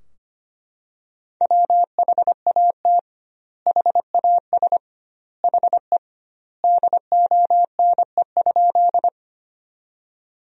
Morse code, heard as a keyed tone, about 25 wpm, 700 Hz. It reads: WHAT HAS HE DONE?
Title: 2x Word-Spacing - Sentences from Top 500 Words - 25wpm